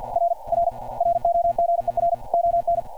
Morse code, heard as a keyed tone, about 52 wpm, 690 Hz. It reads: R R ROGER ON
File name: archived / CRSnet